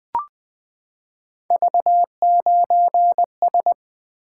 Morse code, heard as a keyed tone, about 20 wpm, 700 Hz. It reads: V9S
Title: Three Letter-Number Combinations - 20wpm